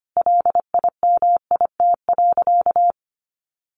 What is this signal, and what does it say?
Morse code, heard as a keyed tone, about 25 wpm, 700 Hz. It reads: LIMIT.
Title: Sentences from Top 1500 Words 25wpm